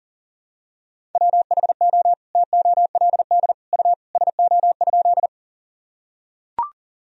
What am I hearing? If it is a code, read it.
Morse code, heard as a keyed tone, about 37 wpm, 700 Hz. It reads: WHO TOLD U SO?
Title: Sentences from Top 400 Words and Abbreviations 40wpm